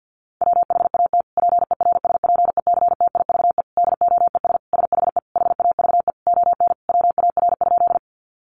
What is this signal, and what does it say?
Morse code, heard as a keyed tone, about 50 wpm, 700 Hz. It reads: WHAT PERSPECTIVE DOES SHE HAVE ON WAR?
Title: Sentences from Top 1200 Words 50wpm